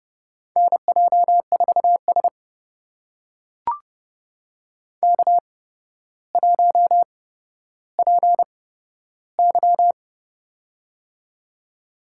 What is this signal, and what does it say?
Morse code, heard as a keyed tone, about 30 wpm, spaced as for 13 wpm, 700 Hz. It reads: NJ4S K1PY
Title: USA Call Signs - 4 Characters - 8x Character Spacing - 30wpm